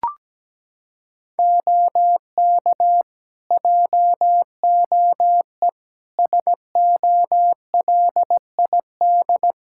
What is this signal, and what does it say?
Morse code, heard as a keyed tone, about 17 wpm, 700 Hz. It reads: OK JOE SOLID
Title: Practice QSO - Exchange 3 - More Info - 17wpm